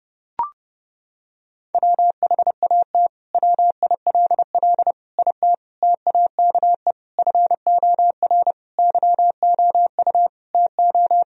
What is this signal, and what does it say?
Morse code, heard as a keyed tone, about 30 wpm, 700 Hz. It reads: WHAT WILL IT TAKE FOR YOU TO
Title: Sentences from Top 1000 Words - Review Part 3 - 30wpm